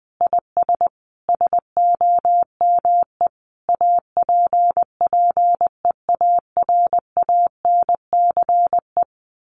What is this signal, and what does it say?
Morse code, heard as a keyed tone, about 20 wpm, 700 Hz. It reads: IS SOME APPEARANCE
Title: Sentences from Top 1100 Words 20wpm